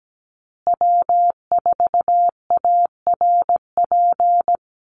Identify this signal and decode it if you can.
Morse code, heard as a keyed tone, about 17 wpm, 700 Hz. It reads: W4ARP
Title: USA Call Signs Repeated 3x 17wpm